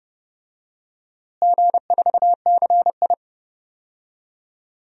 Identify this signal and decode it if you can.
Morse code, heard as a keyed tone, about 30 wpm, 700 Hz. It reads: G4CI
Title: DXCC-100 Call Signs - 4 Characters - Repeated 2x - 30wpm